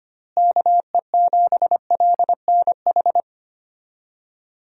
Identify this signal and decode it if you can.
Morse code, heard as a keyed tone, about 25 wpm, 700 Hz. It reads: KE7LNH
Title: USA Call Signs - 6 Characters - Repeated 3x - 25wpm